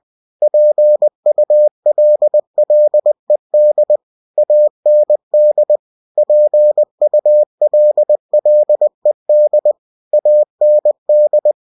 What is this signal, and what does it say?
Morse code, heard as a keyed tone, about 20 wpm, 600 Hz. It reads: PULLED AND PULLED AND
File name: cw-books / winnie-the-pooh / winnie-20wpm-page-0017